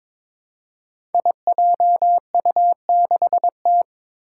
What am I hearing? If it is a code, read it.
Morse code, heard as a keyed tone, about 23 wpm, 700 Hz. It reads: IJU6T
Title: Five Challenging Letter-Number Combinations 22wpm